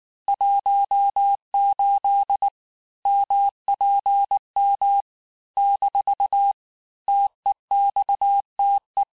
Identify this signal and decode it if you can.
Morse code, heard as a keyed tone, about 19 wpm, 800 Hz. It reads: 18 MPM - TEXTE
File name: cours-du-lundi files / cours-1583--texte-chiffre-18-mpm